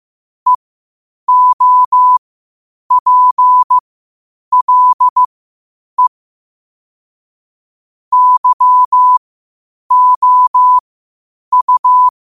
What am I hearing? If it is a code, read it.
Morse code, heard as a keyed tone, about 15 wpm, spaced as for 8 wpm, 1000 Hz. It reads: EOPLE YOU
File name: qotdfiles / 8wpm